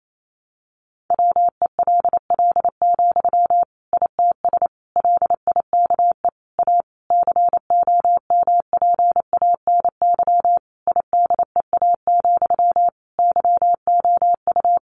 Morse code, heard as a keyed tone, about 28 wpm, 700 Hz. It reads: WELL, ITS LIKE A COMPANY IDEA, YOU
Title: Sentences from Top 600 Words 28wpm